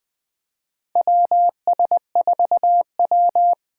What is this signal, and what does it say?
Morse code, heard as a keyed tone, about 21 wpm, 700 Hz. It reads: WS4W
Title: USA Call Signs - 4 Characters - 3x Character Spacing - 20wpm